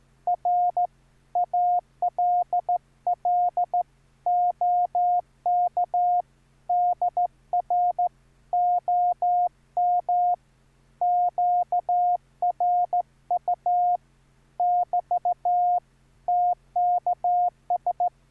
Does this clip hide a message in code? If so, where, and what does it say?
Yes: Morse code from 0.3 to 18.1 s, heard as a keyed tone, about 15 wpm, 710 Hz. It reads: R ALL OK DR OM QRU <BT> TKS